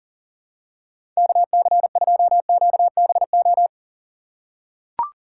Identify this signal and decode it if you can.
Morse code, heard as a keyed tone, about 40 wpm, 700 Hz. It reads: KC2QBO
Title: USA Call Signs - 6 Characters - 3x Character Spacing - 40wpm